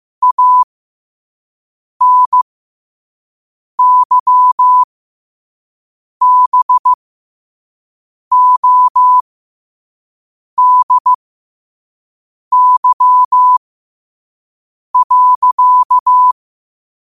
Morse code, heard as a keyed tone, about 15 wpm, spaced as for 5 wpm, 1000 Hz. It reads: ANYBODY.
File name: qotdfiles / 5wpm